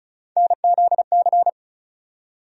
Morse code, heard as a keyed tone, about 36 wpm, 700 Hz. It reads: NZC
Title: USA Call Sign Postfixes - 3 Letters 35wpm